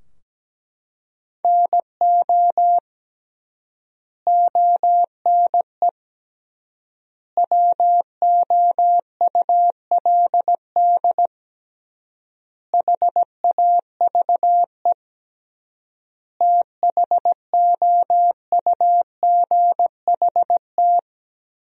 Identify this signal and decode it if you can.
Morse code, heard as a keyed tone, about 17 wpm, 700 Hz. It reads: NO ONE WOULD HAVE THOUGHT
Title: 3x Word-Spacing - Sentences from Top 300 Words - 17wpm